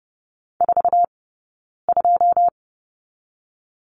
Morse code, heard as a keyed tone, about 30 wpm, 700 Hz. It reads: 4 2
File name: Lesson 15d - Number 5 - Two Characters 30wpm